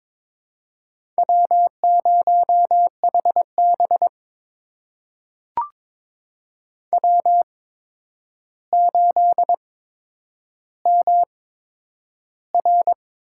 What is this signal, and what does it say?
Morse code, heard as a keyed tone, about 22 wpm, spaced as for 9 wpm, 700 Hz. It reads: W0HB W8MR
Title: USA Call Signs - 4 Characters - 8x Character Spacing - 22wpm